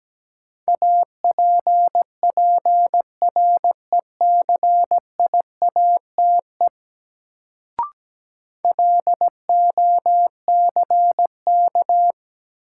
Morse code, heard as a keyed tone, about 17 wpm, 700 Hz. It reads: APPRECIATE LOCK
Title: Sentences from Top 1300 Words - New Word Drill - 17wpm